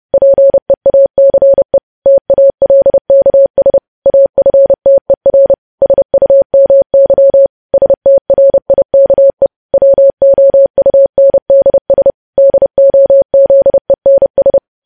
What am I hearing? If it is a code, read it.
Morse code, heard as a keyed tone, about 30 wpm, 550 Hz. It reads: PEACE TALKS AFTER SUMY STRIKE WOUNDS DOZENS
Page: news headlines in morse code at 30 wpm